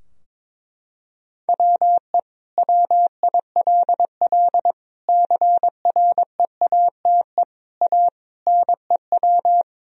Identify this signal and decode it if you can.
Morse code, heard as a keyed tone, about 22 wpm, 700 Hz. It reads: WE WILL CREATE A NEW
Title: Sentences from Top 1000 Words - Review Part 3 - 22wpm